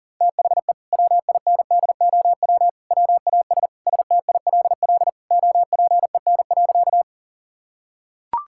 Morse code, heard as a keyed tone, about 40 wpm, 700 Hz. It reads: THE WINDOW WAS STILL OPEN.
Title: Sentences from Top 1600 Words 40wpm